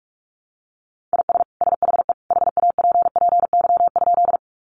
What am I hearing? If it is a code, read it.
Morse code, heard as a keyed tone, about 44 wpm, 700 Hz. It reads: IS SHE HAPPY?
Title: Sentences from Top 1000 Words 45wpm